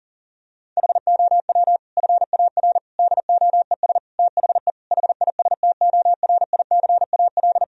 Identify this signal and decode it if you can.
Morse code, heard as a keyed tone, about 40 wpm, 700 Hz. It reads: HOW FAR DOES THE HISTORICAL
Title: Sentences from Top 1000 Words - Review Part 4 - 40wpm